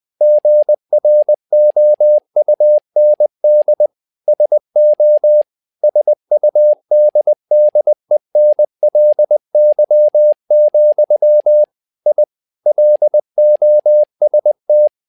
Morse code, heard as a keyed tone, about 20 wpm, 600 Hz. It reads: GROUND SO SUDDENLY, I LOST
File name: tower trouble 20wpm